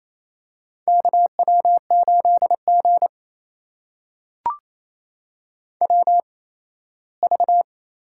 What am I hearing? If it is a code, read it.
Morse code, heard as a keyed tone, about 28 wpm, spaced as for 17 wpm, 700 Hz. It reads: KW8G WV
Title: USA Call Signs - 4 Characters - 8x Character Spacing - 28wpm